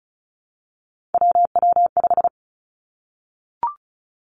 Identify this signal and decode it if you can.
Morse code, heard as a keyed tone, about 36 wpm, 700 Hz. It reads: WW5
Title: Three Challenging Letter-Number Combinations 35wpm